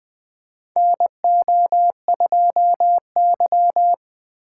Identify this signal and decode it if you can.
Morse code, heard as a keyed tone, about 20 wpm, 700 Hz. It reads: NO2Y
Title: USA Call Signs - 2x Character Spacing - 20wpm